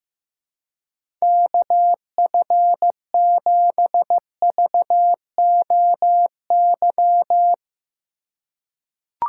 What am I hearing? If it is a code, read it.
Morse code, heard as a keyed tone, about 15 wpm, 700 Hz. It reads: KF7VOY
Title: USA Call Signs - 6 Characters - 2x Character Spacing - 15wpm